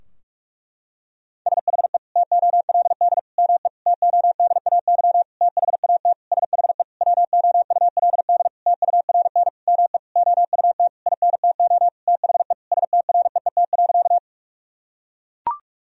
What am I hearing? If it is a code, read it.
Morse code, heard as a keyed tone, about 41 wpm, 700 Hz. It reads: SHE TOLD ME TODAY THAT SHE WOULD TURN ME OUT INTO THE STREET.
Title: Sentences from Top 1000 Words - Review Part 4 - 45wpm